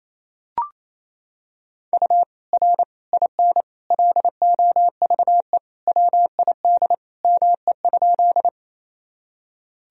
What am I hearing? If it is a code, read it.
Morse code, heard as a keyed tone, about 28 wpm, 700 Hz. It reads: U R IN LOVE WID ME?
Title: Sentences from Top 400 Words and Abbreviations 28wpm